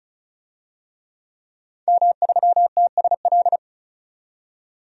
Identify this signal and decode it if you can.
Morse code, heard as a keyed tone, about 36 wpm, 700 Hz. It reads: M3TSL
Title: DXCC-100 Call Signs - 5 Characters - 35wpm